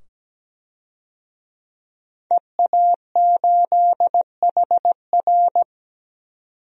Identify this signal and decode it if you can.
Morse code, heard as a keyed tone, about 17 wpm, 700 Hz. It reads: EA8HR
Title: DXCC-100 Call Signs - 5 Characters - 17wpm